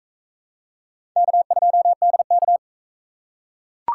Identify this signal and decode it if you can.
Morse code, heard as a keyed tone, about 44 wpm, 700 Hz. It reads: K2DK